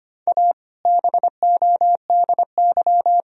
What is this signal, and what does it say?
Morse code, heard as a keyed tone, about 25 wpm, 700 Hz. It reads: A BODY
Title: Sentences from Top 1000 Words - Review Part 3 - 25wpm